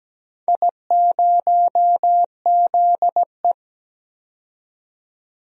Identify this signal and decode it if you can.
Morse code, heard as a keyed tone, about 17 wpm, 700 Hz. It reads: I0ZE
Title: DXCC-100 Call Signs - 4 Characters - 17wpm